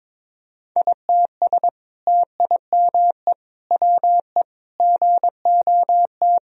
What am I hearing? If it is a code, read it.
Morse code, heard as a keyed tone, about 22 wpm, 700 Hz. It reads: ITS TIME WE GOT
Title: Sentences from Top 1000 Words - Review Part 4 - 22wpm